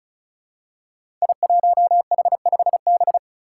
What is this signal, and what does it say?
Morse code, heard as a keyed tone, about 36 wpm, 700 Hz. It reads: I1H5B
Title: Five Challenging Letter-Number Combinations 35wpm